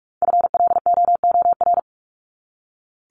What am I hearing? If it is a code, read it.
Morse code, heard as a keyed tone, about 44 wpm, 700 Hz. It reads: FLOOR
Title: Top 1751-2000 Words 45wpm